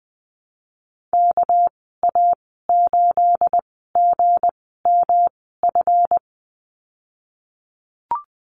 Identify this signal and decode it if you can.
Morse code, heard as a keyed tone, about 20 wpm, 700 Hz. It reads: K A 8 G M F
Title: 238 Lesson 38g - Letter q - Call Signs 20wpm